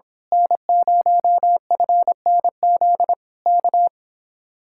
Morse code, heard as a keyed tone, about 26 wpm, 700 Hz. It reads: N0FNZ K
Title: Speed-Racing Common QSO Phrases - 22wpm